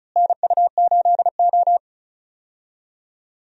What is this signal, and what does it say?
Morse code, heard as a keyed tone, about 36 wpm, 700 Hz. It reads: NU8O
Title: USA Call Signs - 4 Characters - Repeated 3x - 35wpm